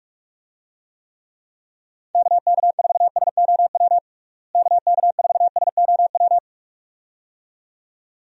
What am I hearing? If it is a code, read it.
Morse code, heard as a keyed tone, about 41 wpm, 700 Hz. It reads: KK4SOW KK4SOW
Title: USA Call Signs - 6 Characters - Repeated 2x - 45wpm